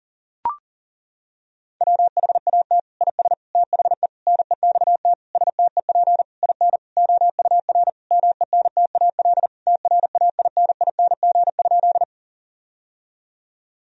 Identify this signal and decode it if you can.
Morse code, heard as a keyed tone, about 37 wpm, 700 Hz. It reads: WHAT IS THE NEXT STEP IN OUR MENTAL TRAINING?
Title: Sentences from Top 1000 Words - Review Part 4 - 40wpm